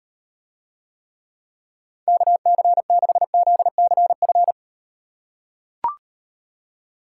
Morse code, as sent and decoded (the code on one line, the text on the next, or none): -.- -.-. -.... --.. -.-. ..-.
KC6ZCF